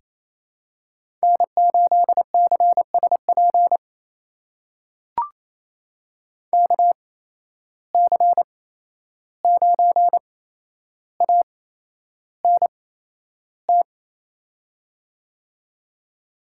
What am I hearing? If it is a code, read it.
Morse code, heard as a keyed tone, about 28 wpm, spaced as for 11 wpm, 700 Hz. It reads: N8CSP KC9ANT
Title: USA Call Signs - 8x Character Spacing - 28wpm